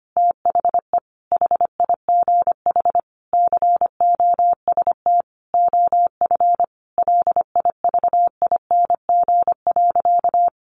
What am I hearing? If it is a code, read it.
Morse code, heard as a keyed tone, about 25 wpm, 700 Hz. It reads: THE HIGH COST OF LIVING.
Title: Sentences from Top 1400 Words 25wpm